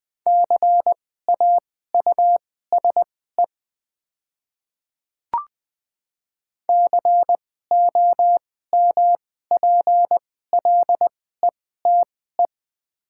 Morse code, heard as a keyed tone, about 20 wpm, spaced as for 14 wpm, 700 Hz. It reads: CAUSE COMPLETE
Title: 2x Character Spacing - Top 500 Words - 20wpm